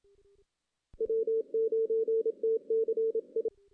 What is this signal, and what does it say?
Morse code, heard as a keyed tone, about 27 wpm, 450 Hz. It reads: W9TCI
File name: 2016 JAN CW NAQP / 2016 Jan naqp k5ot - 0-6